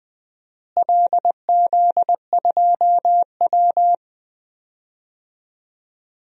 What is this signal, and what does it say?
Morse code, heard as a keyed tone, about 20 wpm, 700 Hz. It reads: LZ2W
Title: DXCC-100 Call Signs - 4 Characters - Repeated 2x - 20wpm